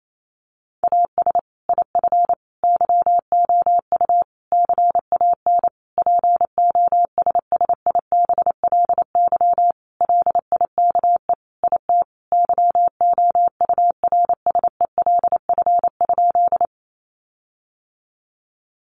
Morse code, heard as a keyed tone, about 28 wpm, 700 Hz. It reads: AS IF YOU CAN POSSIBLY LIKE IT YOURSELF?
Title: Sentences from Top 1300 Words 28wpm